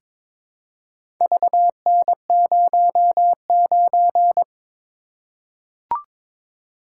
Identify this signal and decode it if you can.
Morse code, heard as a keyed tone, about 22 wpm, 700 Hz. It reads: VN09
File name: practice / 22wpm / Four Letter-Number Combinations - 22wpm